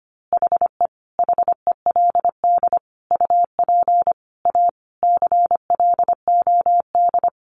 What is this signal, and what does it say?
Morse code, heard as a keyed tone, about 25 wpm, 700 Hz. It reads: HE HELD UP A CLOD
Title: A Sound Of Thunder 25wpm